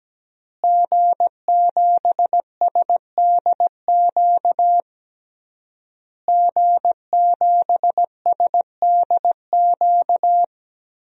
Morse code, heard as a keyed tone, about 17 wpm, 700 Hz. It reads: G7SDQ G7SDQ
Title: DXCC-100 Call Signs - 5 Characters - Repeated 3x - 17wpm